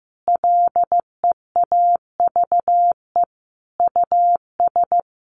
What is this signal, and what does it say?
Morse code, heard as a keyed tone, about 15 wpm, 700 Hz. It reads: LEAVE US
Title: Sentences from Top 1100 Words 15wpm